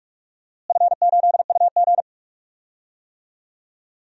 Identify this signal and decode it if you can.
Morse code, heard as a keyed tone, about 44 wpm, 700 Hz. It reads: F8UG
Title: DXCC-100 Call Signs - 4 Characters - Repeated 2x - 45wpm